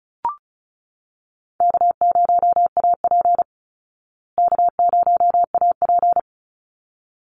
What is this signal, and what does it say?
Morse code, heard as a keyed tone, about 34 wpm, 700 Hz. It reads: K0AP K0AP
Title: Easy Call Signs Repeated 2x - 35wpm